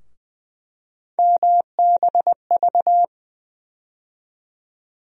Morse code, heard as a keyed tone, about 21 wpm, 700 Hz. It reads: MBV
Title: USA Call Sign Postfixes - 3 Letters 20wpm